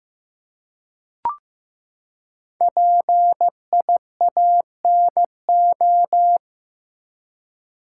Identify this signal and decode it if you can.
Morse code, heard as a keyed tone, about 15 wpm, 700 Hz. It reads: PIANO
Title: Five Letter Words - 15wpm